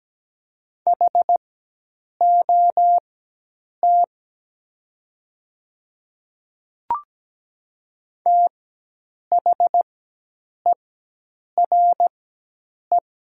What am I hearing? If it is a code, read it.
Morse code, heard as a keyed tone, about 17 wpm, spaced as for 8 wpm, 700 Hz. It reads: HOT THERE
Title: 4x Character Spacing - Top 100 Words - 17wpm